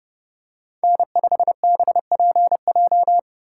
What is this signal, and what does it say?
Morse code, heard as a keyed tone, about 30 wpm, 700 Hz. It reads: N5BPJ
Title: USA Call Signs - 5 Characters - 4x Character Spacing - 30wpm